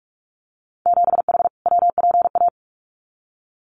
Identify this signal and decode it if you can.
Morse code, heard as a keyed tone, about 44 wpm, 700 Hz. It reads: 7H WPA
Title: Winter Field Day Exchanges Encore 45wpm